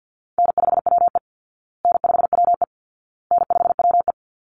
Morse code, heard as a keyed tone, about 50 wpm, 700 Hz. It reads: N5WE N5WE N5WE
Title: Call Signs Repeated 2x - 50wpm